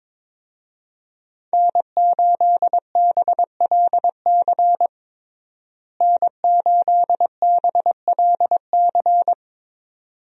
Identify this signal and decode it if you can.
Morse code, heard as a keyed tone, about 22 wpm, 700 Hz. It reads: N8BLC N8BLC
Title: USA Call Signs - 5 Characters - Repeated 2x - 22wpm